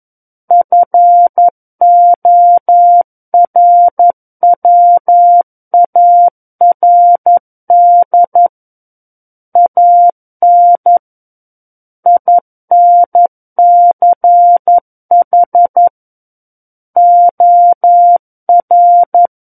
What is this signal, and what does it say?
Morse code, heard as a keyed tone, about 11 wpm, 700 Hz. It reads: FORWARD AN INCH OR
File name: Book One / 02-Chapter 2 - 11 WPM